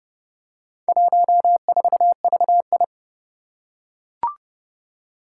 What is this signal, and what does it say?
Morse code, heard as a keyed tone, about 30 wpm, 700 Hz. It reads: 14VI
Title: Four Challenging Letter-Number Combinations 30wpm